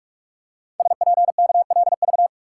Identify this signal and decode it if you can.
Morse code, heard as a keyed tone, about 44 wpm, 700 Hz. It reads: SPKLV
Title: Code Groups - 5 Letters - 45wpm